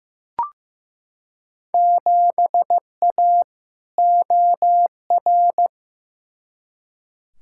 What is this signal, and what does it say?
Morse code, heard as a keyed tone, about 16 wpm, 700 Hz. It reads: 7A OR
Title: Field Day Exchanges Encore - 15wpm